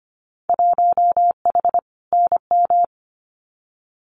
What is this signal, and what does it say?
Morse code, heard as a keyed tone, about 25 wpm, 700 Hz. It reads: 1H NM
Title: Winter Field Day Exchanges 25wpm